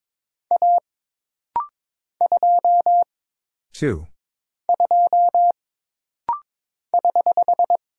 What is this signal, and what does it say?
Morse code, heard as a keyed tone, about 22 wpm, 700 Hz. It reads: A 2 2 <HH>
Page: This rapid-fire timing is geared to push your ICR (Instant Character Recognition) skills.